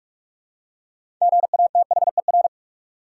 Morse code, heard as a keyed tone, about 42 wpm, 700 Hz. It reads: GATHER
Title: Six Letter Words - 45wpm